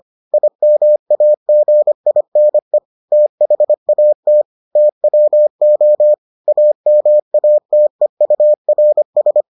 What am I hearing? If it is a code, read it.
Morse code, heard as a keyed tone, about 25 wpm, 600 Hz. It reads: IMAGINE THAT TWO AMATEURS